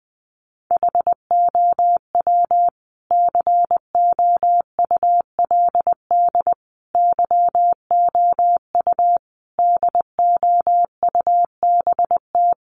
Morse code, heard as a keyed tone, about 20 wpm, 700 Hz. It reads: HOW COULD YOU DOUBT
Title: Sentences from Top 1500 Words 20wpm